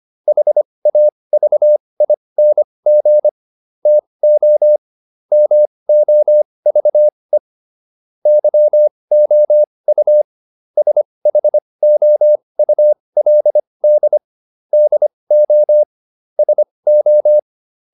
Morse code, heard as a keyed tone, about 25 wpm, spaced as for 20 wpm, 600 Hz. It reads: HAVING TO MOVE YOU SHOULD DO SO
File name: SS209 20